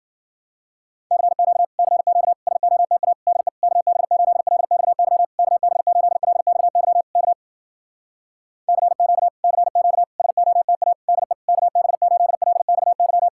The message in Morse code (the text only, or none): CQ CQ SOTA DE KB8LXY KB8LXY K CQ CQ SOTA DE KB8LXY